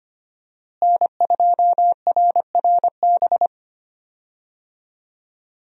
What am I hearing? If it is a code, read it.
Morse code, heard as a keyed tone, about 25 wpm, 700 Hz. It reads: N2RRB
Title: USA Call Signs - 5 Characters - Repeated 3x - 25wpm